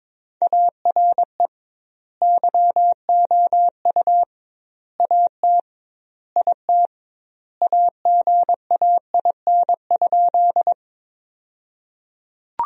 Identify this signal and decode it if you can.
Morse code, heard as a keyed tone, about 22 wpm, 700 Hz. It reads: ARE YOU AT IT AGAIN?
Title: 2x Word-Spacing - Sentences from Top 200 Words - 22wpm